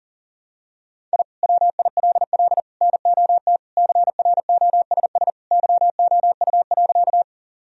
Morse code, heard as a keyed tone, about 37 wpm, 700 Hz. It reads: I WILL NOT CROSS YOU.
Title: Sentences from Top 1000 Words - Review Part 1 - 40wpm